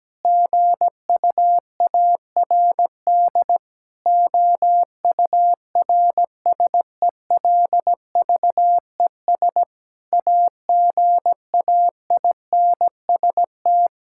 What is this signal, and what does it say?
Morse code, heard as a keyed tone, about 17 wpm, 700 Hz. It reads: GUARD OURSELVES AGAINST
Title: Sentences from Top 1600 Words 17wpm